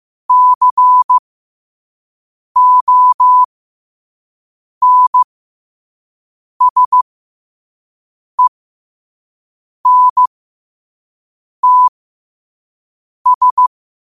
Morse code, heard as a keyed tone, about 15 wpm, spaced as for 5 wpm, 1000 Hz. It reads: CONSENTS
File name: qotdfiles / 5wpm